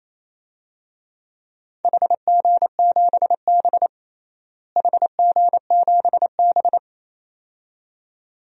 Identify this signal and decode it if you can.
Morse code, heard as a keyed tone, about 28 wpm, 700 Hz. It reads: HG7B HG7B
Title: DXCC-100 Call Signs - 4 Characters - Repeated 2x - 28wpm